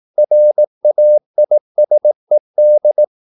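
Morse code, heard as a keyed tone, about 18 wpm, 600 Hz. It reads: RAISED
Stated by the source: Morse practice at 18 wpm, English text, from CW med Gnister.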